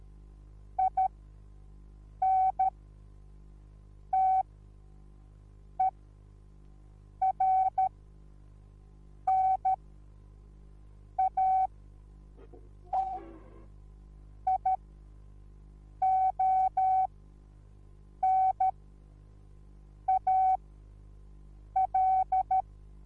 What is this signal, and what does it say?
Morse code, heard as a keyed tone, about 13 wpm, spaced as for 5 wpm, 760 Hz. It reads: INTERNATIONAL
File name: Onlinelehrgang / morsen / DJ4UF Morsen21